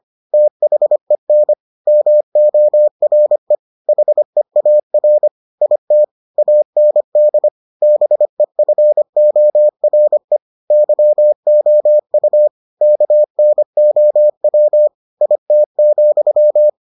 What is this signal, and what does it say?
Morse code, heard as a keyed tone, about 25 wpm, 600 Hz. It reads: THEN MORE HEAR IT AND BEFORE YOU KNOW IT,